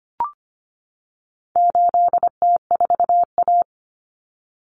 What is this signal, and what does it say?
Morse code, heard as a keyed tone, about 25 wpm, 700 Hz. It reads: 8T4A
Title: Common Cut Numbers - 4-Digit 25wpm